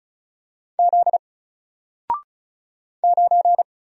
Morse code, heard as a keyed tone, about 34 wpm, 700 Hz. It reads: Z 9
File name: Single-Letter-Number-35wpm